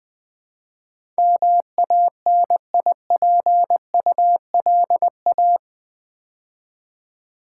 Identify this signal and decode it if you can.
Morse code, heard as a keyed tone, about 20 wpm, 700 Hz. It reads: MANIPULA
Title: Build A Word - 4 Letters At A Time - 12-Letter Words - 20wpm